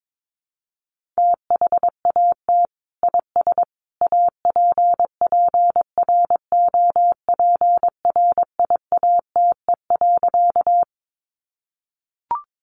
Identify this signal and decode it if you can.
Morse code, heard as a keyed tone, about 22 wpm, 700 Hz. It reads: THAT IS APPROPRIATE.
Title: Sentences from Top 900 Words 22wpm